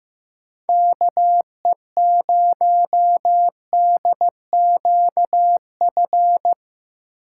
Morse code, heard as a keyed tone, about 15 wpm, 700 Hz. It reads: KE0DQF
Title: USA Call Signs - 6 Characters - 2x Character Spacing - 15wpm